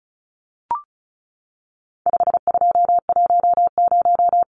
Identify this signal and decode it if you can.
Morse code, heard as a keyed tone, about 34 wpm, 700 Hz. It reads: 5210
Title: Numbers - 4-Digit 35wpm